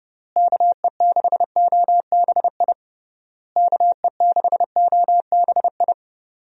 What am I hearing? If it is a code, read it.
Morse code, heard as a keyed tone, about 30 wpm, 700 Hz. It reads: KE6OBI KE6OBI
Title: USA Call Signs - 6 Characters - Repeated 3x - 30wpm